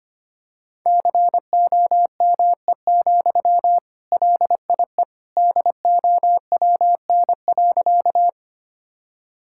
Morse code, heard as a keyed tone, about 25 wpm, 700 Hz. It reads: COME, LIE DOWN.
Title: Sentences from Top 1600 Words 25wpm